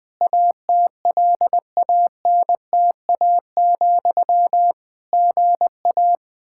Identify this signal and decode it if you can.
Morse code, heard as a keyed tone, about 20 wpm, 700 Hz. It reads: ATLANTA, GA
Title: Practice QSO - Exchange 1 - Beginning - 20wpm